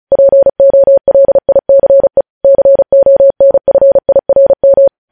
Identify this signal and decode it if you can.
Morse code, heard as a keyed tone, about 36 wpm, 550 Hz. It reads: POLICE CONFIRM
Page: News Headlines in Morse Code at 35 WPM